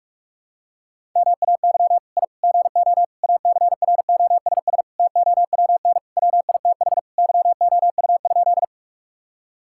Morse code, heard as a keyed tone, about 46 wpm, 700 Hz. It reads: MAY I GO ACROSS TOWN WITH YOU?
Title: Sentences from Top 1000 Words - Review Part 5 - 45wpm